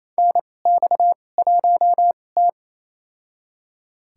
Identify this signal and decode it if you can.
Morse code, heard as a keyed tone, about 28 wpm, spaced as for 20 wpm, 700 Hz. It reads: NX1T
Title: USA Call Signs - 4 Characters - 2x Character Spacing - 28wpm